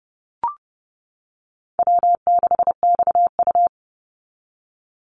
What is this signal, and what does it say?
Morse code, heard as a keyed tone, about 29 wpm, 700 Hz. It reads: W6XU
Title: Easy Call Signs - 30wpm